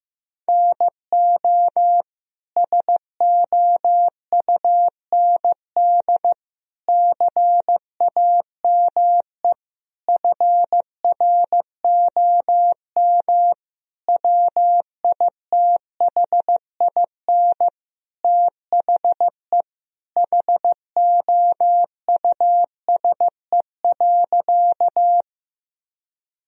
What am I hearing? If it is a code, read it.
Morse code, heard as a keyed tone, about 15 wpm, 700 Hz. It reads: NO SOUND CAME FROM WITHIN THE HOUSE.
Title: Sentences from Top 1000 Words - Review Part 4 - 15wpm